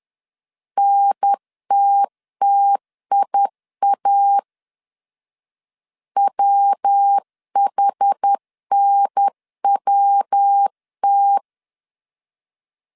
Morse code, heard as a keyed tone, about 10 wpm, 800 Hz. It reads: NTTIA WHNWT
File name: 10wpm